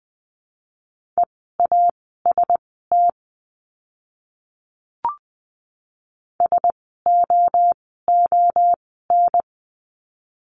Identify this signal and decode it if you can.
Morse code, heard as a keyed tone, about 20 wpm, spaced as for 14 wpm, 700 Hz. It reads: EAST SOON
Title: 068 Lesson 12f - Letter d - Words 20wpm